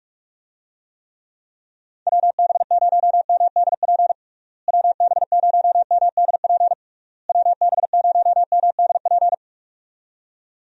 Morse code, heard as a keyed tone, about 41 wpm, 700 Hz. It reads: WB0MDP WB0MDP WB0MDP
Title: USA Call Signs - 6 Characters - Repeated 3x - 45wpm